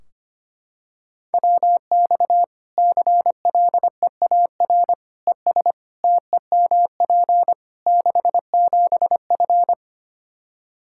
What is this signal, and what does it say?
Morse code, heard as a keyed tone, about 25 wpm, 700 Hz. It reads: WX CLEAR ES TEMP 67F
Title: K1USN SST QSO Practice 25wpm